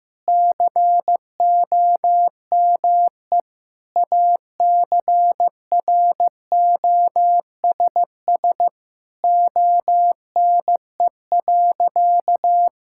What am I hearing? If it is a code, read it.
Morse code, heard as a keyed tone, about 16 wpm, 700 Hz. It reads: COME ACROSS ONE.
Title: Sentences from Top 1000 Words - Review Part 5 - 15wpm